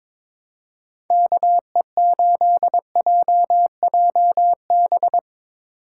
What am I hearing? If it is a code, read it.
Morse code, heard as a keyed tone, about 22 wpm, 700 Hz. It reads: KE8JJB
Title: USA Call Signs - 6 Characters - 8x Character Spacing - 22wpm